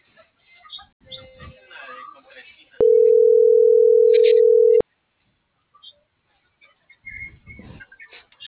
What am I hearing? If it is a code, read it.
Ringback tone (North America), 440 + 480 Hz.